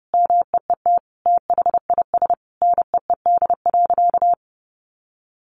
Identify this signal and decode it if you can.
Morse code, heard as a keyed tone, about 30 wpm, 700 Hz. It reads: MEET THIS NEED.
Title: Sentences from Top 1100 Words 30wpm